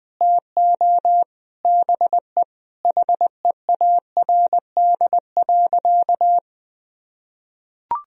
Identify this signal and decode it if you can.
Morse code, heard as a keyed tone, about 20 wpm, 700 Hz. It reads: TO BE HEARD.
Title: Sentences from Top 1000 Words - Review Part 1 - 20wpm